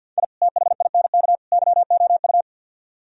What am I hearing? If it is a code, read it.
Morse code, heard as a keyed tone, about 61 wpm, 700 Hz. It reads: I THINK YOU
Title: Speed-Racing - Sets of 3 Words 50wpm